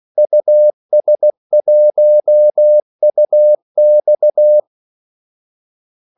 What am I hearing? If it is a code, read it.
Morse code, heard as a keyed tone, about 16 wpm, 600 Hz. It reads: US1UX
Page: Kaldesignaler 16 wpm